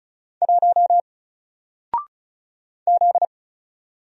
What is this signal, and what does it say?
Morse code, heard as a keyed tone, about 36 wpm, 700 Hz. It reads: 1 Z
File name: Single-Letter-Number-35wpm